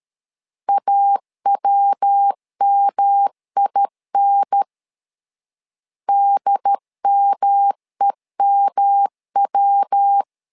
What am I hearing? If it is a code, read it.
Morse code, heard as a keyed tone, about 13 wpm, 800 Hz. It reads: AWMIN DMEMW
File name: LEZIONI / 3 MWRO / 3 MW 12wpm 1633